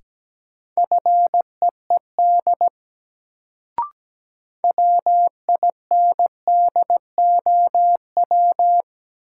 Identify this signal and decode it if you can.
Morse code, heard as a keyed tone, about 17 wpm, 700 Hz. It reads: FEED WINDOW
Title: Sentences from Top 1600 Words - New Word Drill - 17wpm